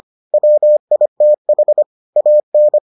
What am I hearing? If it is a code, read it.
Morse code, heard as a keyed tone, about 25 wpm, 600 Hz. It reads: WITH AN